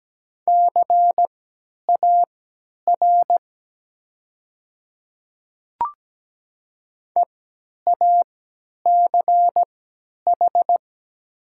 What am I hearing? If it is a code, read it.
Morse code, heard as a keyed tone, about 17 wpm, spaced as for 10 wpm, 690 Hz. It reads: CAR EACH
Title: 3x Character Spacing - Top 300 Words - 17wpm